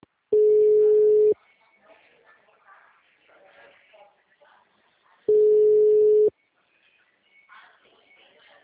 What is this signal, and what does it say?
Ringback tone (Europe, 425 Hz), with speech.